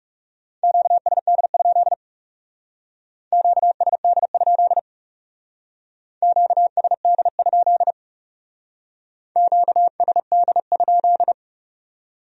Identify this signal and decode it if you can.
Morse code, heard as a keyed tone, about 39 wpm, 700 Hz. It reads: QSD? QSD? QSD? QSD?